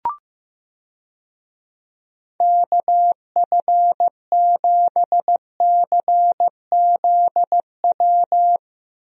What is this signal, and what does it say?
Morse code, heard as a keyed tone, about 15 wpm, 700 Hz. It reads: KF7CZW
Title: USA Call Signs - 6 Characters - Repeated 3x - 15wpm